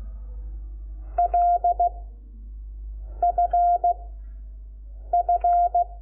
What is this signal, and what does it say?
Morse code, heard as a keyed tone, about 16 wpm, 700 Hz. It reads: L F F
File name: Lecciones / CLASE Nº5